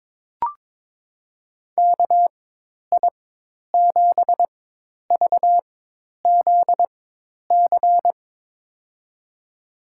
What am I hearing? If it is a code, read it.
Morse code, heard as a keyed tone, about 22 wpm, spaced as for 10 wpm, 700 Hz. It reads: KI7VZC
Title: USA Call Signs - 6 Characters - 4x Character Spacing - 22wpm